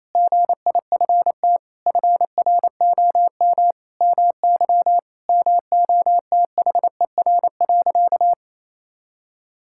Morse code, heard as a keyed tone, about 27 wpm, 700 Hz. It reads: GIFT FROM MY MOTHER.
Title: Sentences from Top 1000 Words - Review Part 6 - 28wpm